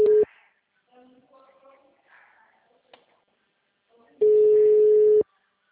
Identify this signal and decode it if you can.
Ringback tone (Europe, 425 Hz).
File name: out- -2016